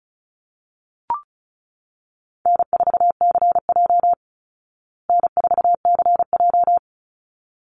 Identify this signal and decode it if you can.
Morse code, heard as a keyed tone, about 34 wpm, 700 Hz. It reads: N4CJ N4CJ
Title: Easy Call Signs Repeated 2x - 35wpm